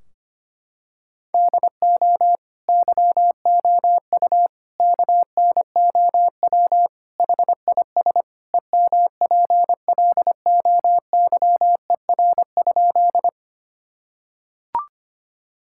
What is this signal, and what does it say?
Morse code, heard as a keyed tone, about 25 wpm, 700 Hz. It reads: DO YOU KNOW HIS EMPLOYER?
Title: Sentences from Top 1600 Words 25wpm